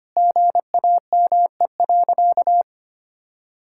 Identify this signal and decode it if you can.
Morse code, heard as a keyed tone, about 25 wpm, 700 Hz. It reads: GAME.
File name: practice / 25wpm / Sentences with 3-Word Phrases - 25wpm